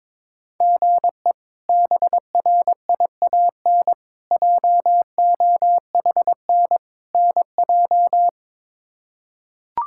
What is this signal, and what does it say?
Morse code, heard as a keyed tone, about 22 wpm, 700 Hz. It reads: GE BRIAN JOHN NJ
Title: K1USN SST QSO Practice - 22wpm